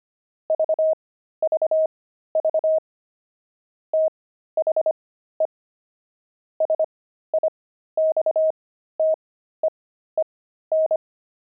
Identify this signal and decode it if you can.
Morse code, heard as a keyed tone, about 25 wpm, spaced as for 13 wpm, 630 Hz. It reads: VVV THE SIXTEEN
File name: pangram-fs-13-cs-25-beehive